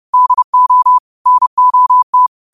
Morse code, heard as a keyed tone, about 30 wpm, 1000 Hz. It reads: DO NOT